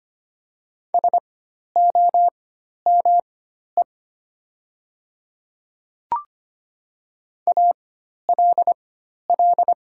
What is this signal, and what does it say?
Morse code, heard as a keyed tone, about 25 wpm, spaced as for 12 wpm, 700 Hz. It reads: SOME ALL
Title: 4x Character Spacing - Top 100 Words - 25wpm